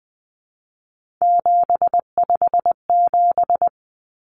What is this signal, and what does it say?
Morse code, heard as a keyed tone, about 20 wpm, 700 Hz. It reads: 757
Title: Common Cut Numbers - 3-Digit 20wpm